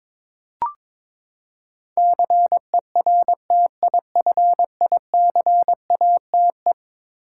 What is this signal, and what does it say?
Morse code, heard as a keyed tone, about 22 wpm, 700 Hz. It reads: CERTIFICATE
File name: practice / 22wpm / Eleven Letter Words - 22wpm